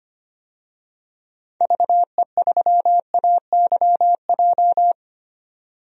Morse code, heard as a keyed tone, about 25 wpm, 700 Hz. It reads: VE3AYJ
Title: DXCC-100 Call Signs - 6 Characters - 25wpm